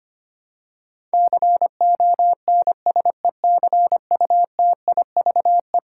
Morse code, heard as a keyed tone, about 25 wpm, 700 Hz. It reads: CONSECUTIVE
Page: morse-code-ninja-practice-25wpm